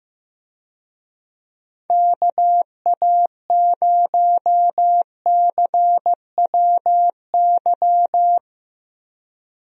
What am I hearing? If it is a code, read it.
Morse code, heard as a keyed tone, about 16 wpm, 700 Hz. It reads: KA0CWY
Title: USA Call Signs - 6 Characters - Repeated 3x - 15wpm